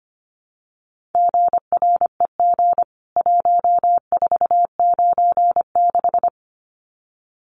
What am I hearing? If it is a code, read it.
Morse code, heard as a keyed tone, about 25 wpm, 700 Hz. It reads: GREG 1496
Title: CWT Exchange 25wpm